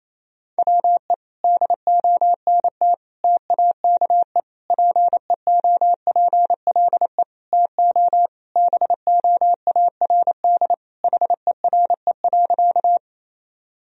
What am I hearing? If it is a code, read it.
Morse code, heard as a keyed tone, about 27 wpm, 700 Hz. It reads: WE DONT TAKE PEOPLE TO BOARD HERE.
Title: Sentences from Top 1000 Words - Review Part 4 - 28wpm